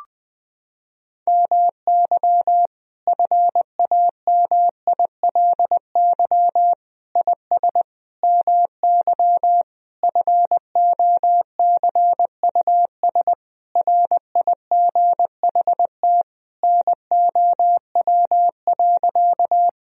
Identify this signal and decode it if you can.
Morse code, heard as a keyed tone, about 20 wpm, 700 Hz. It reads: MY FAMILY IS MY FOCUS RIGHT NOW.
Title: Sentences from Top 1000 Words - Review Part 4 - 20wpm